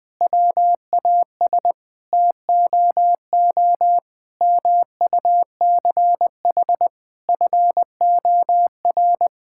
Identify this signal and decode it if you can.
Morse code, heard as a keyed tone, about 21 wpm, 700 Hz. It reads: WAS TOO MUCH FOR
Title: Sentences from Top 1000 Words - Review Part 5 - 20wpm